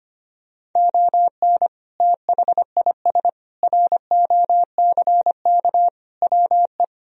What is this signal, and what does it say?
Morse code, heard as a keyed tone, about 25 wpm, 700 Hz. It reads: ON THIS ROCK WE
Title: Sentences from Top 1000 Words - Review Part 2 - 25wpm